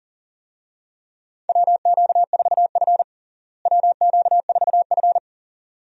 Morse code, heard as a keyed tone, about 40 wpm, 700 Hz. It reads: WQ4F WQ4F
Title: USA Call Signs - 4 Characters - Repeated 2x - 40wpm